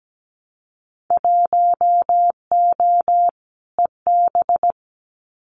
Morse code, heard as a keyed tone, about 17 wpm, 700 Hz. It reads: 1O EB
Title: Winter Field Day Exchanges 17wpm